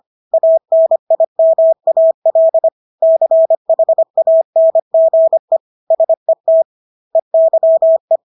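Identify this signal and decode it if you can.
Morse code, heard as a keyed tone, about 25 wpm, 650 Hz. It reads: ANIMAL CHANGE SET EYE
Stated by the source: Morse practice of English words at 25 wpm from CW med Gnister.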